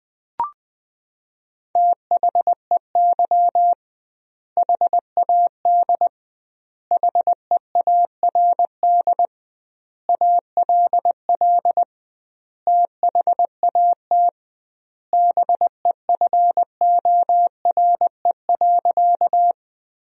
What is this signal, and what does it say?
Morse code, heard as a keyed tone, about 20 wpm, 700 Hz. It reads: THEY HAD HEARD ALL THAT BEFORE.
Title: 2x Word-Spacing - Sentences from Top 400 Words - 20wpm